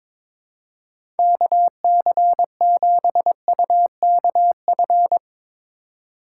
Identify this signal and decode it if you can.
Morse code, heard as a keyed tone, about 22 wpm, 700 Hz. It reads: KC7UKF
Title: USA Call Signs - 2x Character Spacing - 22wpm